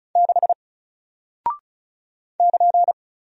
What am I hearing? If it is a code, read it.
Morse code, heard as a keyed tone, about 36 wpm, 700 Hz. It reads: 6 <KN>
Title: Single Character (Letter:Number:Punctuation:Prosign) Alt 35wpm